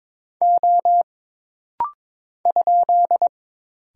Morse code, heard as a keyed tone, about 22 wpm, 700 Hz. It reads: O ?
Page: This rapid-fire timing is geared to push your ICR (Instant Character Recognition) skills.